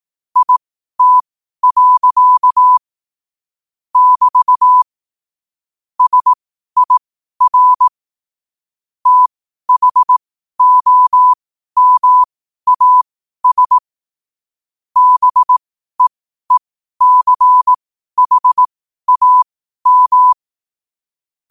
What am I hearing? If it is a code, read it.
Morse code, heard as a keyed tone, about 18 wpm, spaced as for 12 wpm, 1000 Hz. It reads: IT. <BT> SIR THOMAS BEECHAM
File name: qotdfiles / 12wpm